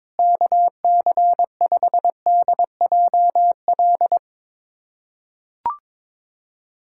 Morse code, heard as a keyed tone, about 22 wpm, 700 Hz. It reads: KC5DJL
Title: USA Callsign DE Callsign 22wpm